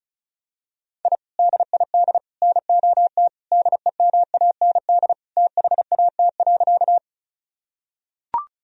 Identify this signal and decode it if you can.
Morse code, heard as a keyed tone, about 36 wpm, 700 Hz. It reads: I DID NOT DEMAND THAT.
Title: Sentences from Top 1000 Words - Review Part 4 - 35wpm